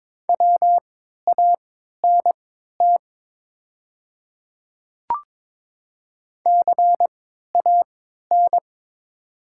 Morse code, heard as a keyed tone, about 22 wpm, spaced as for 12 wpm, 700 Hz. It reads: WANT CAN
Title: 3x Character Spacing - Top 200 Words - 22wpm